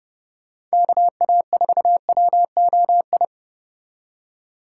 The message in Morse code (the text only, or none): KA4WOI